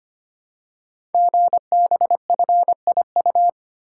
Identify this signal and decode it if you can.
Morse code, heard as a keyed tone, about 25 wpm, 700 Hz. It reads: GBFIU
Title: Code Groups - 5 Letters - 25wpm